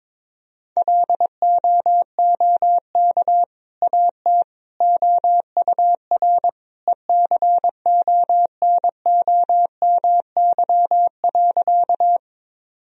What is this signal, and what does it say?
Morse code, heard as a keyed tone, about 22 wpm, 700 Hz. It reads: LOOK AT OUR ECONOMY.
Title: Sentences from Top 1000 Words - Review Part 3 - 22wpm